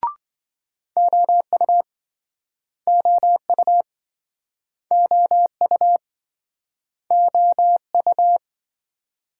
Morse code, heard as a keyed tone, about 27 wpm, 700 Hz. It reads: OU OU OU OU